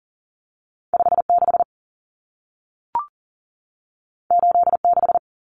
Morse code, heard as a keyed tone, about 40 wpm, 700 Hz. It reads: 56 86
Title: Numbers - 2-Digit 40wpm